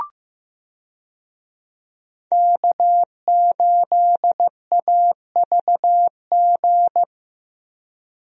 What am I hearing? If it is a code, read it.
Morse code, heard as a keyed tone, about 15 wpm, 700 Hz. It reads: K8AVG
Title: USA Call Signs - 5 Characters - Repeated 3x - 15wpm